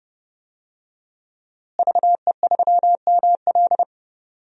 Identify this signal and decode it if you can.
Morse code, heard as a keyed tone, about 30 wpm, 700 Hz. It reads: VE3ML
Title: DXCC-100 Call Signs - 5 Characters - Repeated 3x - 30wpm